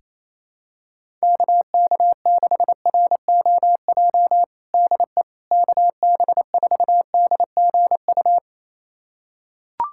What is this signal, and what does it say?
Morse code, heard as a keyed tone, about 27 wpm, 700 Hz. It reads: KK6ROJ DE KB4DGU
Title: USA Callsign DE Callsign - 6 Characters - 28wpm